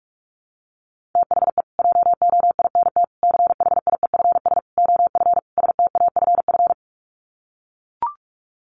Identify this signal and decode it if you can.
Morse code, heard as a keyed tone, about 44 wpm, 700 Hz. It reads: THE JOINT CHIEFS OF STAFF
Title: Sets of 5 Words Encore2 45wpm